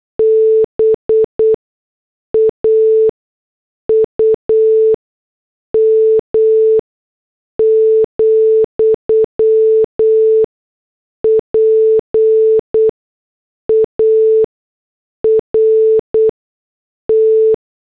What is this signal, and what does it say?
Morse code, heard as a keyed tone, about 8 wpm, spaced as for 6 wpm, 440 Hz. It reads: BAUM, PART
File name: morse / headlines